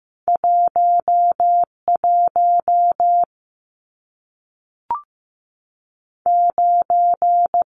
Morse code, heard as a keyed tone, about 15 wpm, 700 Hz. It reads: 1 1 9
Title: Numbers - 2-Digit 15wpm